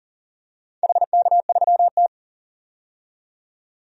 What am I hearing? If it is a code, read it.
Morse code, heard as a keyed tone, about 40 wpm, 700 Hz. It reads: HK3T
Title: DXCC-100 Call Signs - 4 Characters - 40wpm